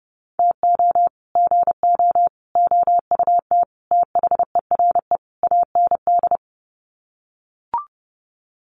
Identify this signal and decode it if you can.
Morse code, heard as a keyed tone, about 30 wpm, 700 Hz. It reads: TO GO OUT THERE AND
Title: Sets of 5 Words Encore2 30wpm